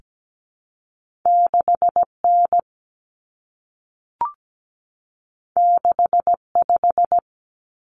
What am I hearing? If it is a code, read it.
Morse code, heard as a keyed tone, about 17 wpm, 700 Hz. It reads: 6N 65
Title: Common Cut Numbers - 2-Digit 17wpm